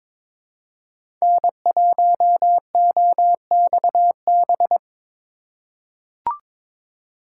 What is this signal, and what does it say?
Morse code, heard as a keyed tone, about 21 wpm, 700 Hz. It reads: N1OXB
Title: USA Call Signs - 5 Characters - 2x Character Spacing - 22wpm